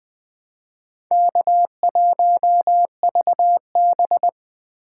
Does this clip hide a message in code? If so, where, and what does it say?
Yes: Morse code from 1.1 to 4.3 s, heard as a keyed tone, about 21 wpm, 700 Hz. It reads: K1VB